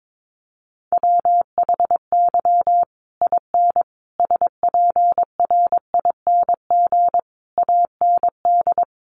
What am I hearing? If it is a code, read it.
Morse code, heard as a keyed tone, about 22 wpm, 700 Hz. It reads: WHY IN SPRING AND
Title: Sentences from Top 1200 Words 22wpm